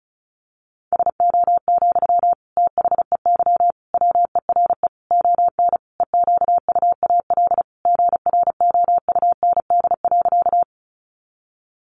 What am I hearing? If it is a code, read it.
Morse code, heard as a keyed tone, about 34 wpm, 700 Hz. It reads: SO, THEY WERE ON EQUAL GROUND.
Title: Sentences from Top 1400 Words 35wpm